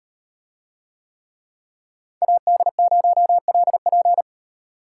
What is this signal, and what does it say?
Morse code, heard as a keyed tone, about 36 wpm, 700 Hz. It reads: AD0LP